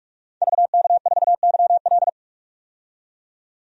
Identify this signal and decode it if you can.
Morse code, heard as a keyed tone, about 42 wpm, 700 Hz. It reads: VK4YL
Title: DXCC-100 Call Signs - 5 Characters - 45wpm